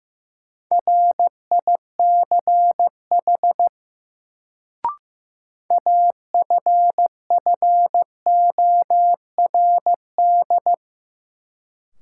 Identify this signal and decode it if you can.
Morse code, heard as a keyed tone, about 15 wpm, 700 Hz. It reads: RICH AFFORD
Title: Sentences from Top 1100 Words - New Word Drill - 15wpm